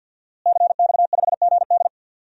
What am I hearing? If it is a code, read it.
Morse code, heard as a keyed tone, about 51 wpm, 700 Hz. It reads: CX5GD
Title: DXCC-100 Call Signs - 5 Characters - Repeated 2x - 50wpm